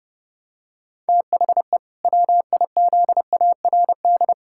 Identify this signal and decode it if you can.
Morse code, heard as a keyed tone, about 30 wpm, 700 Hz. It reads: THE WIZARD
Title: Pangram Challenge v2 - 30wpm